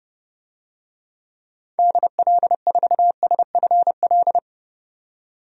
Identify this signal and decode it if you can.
Morse code, heard as a keyed tone, about 30 wpm, 700 Hz. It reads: DL4SFL
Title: DXCC-100 Call Signs - 6 Characters - 30wpm